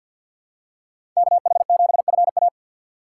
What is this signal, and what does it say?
Morse code, heard as a keyed tone, about 52 wpm, 700 Hz. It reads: KH7FA
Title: USA Call Signs - 5 Characters - 4x Character Spacing - 50wpm